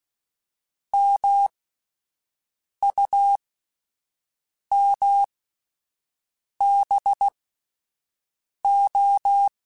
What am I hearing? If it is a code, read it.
Morse code, heard as a keyed tone, about 16 wpm, spaced as for 5 wpm, 800 Hz. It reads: MUMBO
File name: Misc / Misc-mp3s / Word Pairs @5